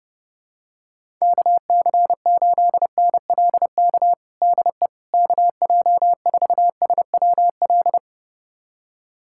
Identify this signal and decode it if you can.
Morse code, heard as a keyed tone, about 32 wpm, 700 Hz. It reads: KC8NLK DE KJ4SWL
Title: USA Callsign DE Callsign 30wpm